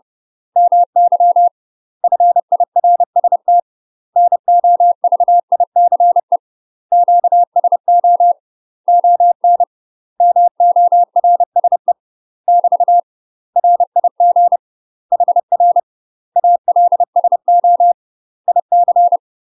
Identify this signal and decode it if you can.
Morse code, heard as a keyed tone, about 30 wpm, 700 Hz. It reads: MY FIRST NOVICE QSO ON MORSE <BT> RIG HR ALSO IC
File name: UK MASTER QSOs TRAINING / 30WPM / UK MASTER QSOs 0048